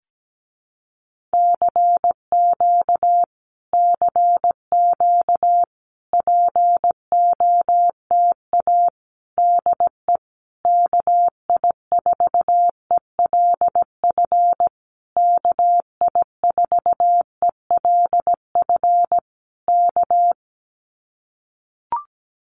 Morse code, heard as a keyed tone, about 17 wpm, 700 Hz. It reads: CQ CQ POTA DE KI4ELF KI4ELF K
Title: Common QSO Phrases Encore 17wpm